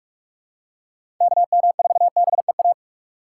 Morse code, heard as a keyed tone, about 45 wpm, 700 Hz. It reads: KM4BEA
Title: USA Call Signs - 6 Characters - 4x Character Spacing - 45wpm